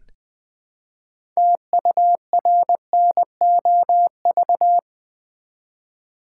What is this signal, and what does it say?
Morse code, heard as a keyed tone, about 20 wpm, 700 Hz. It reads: TURNOV
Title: Build A Word - 2 Letters At A Time - 8-Letter Words - 20wpm